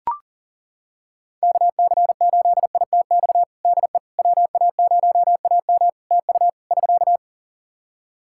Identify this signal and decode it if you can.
Morse code, heard as a keyed tone, about 40 wpm, 700 Hz. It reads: KC8ITX DE WA0AM TU <SK>
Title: Practice QSO - Exchange 4 - Ending - 40wpm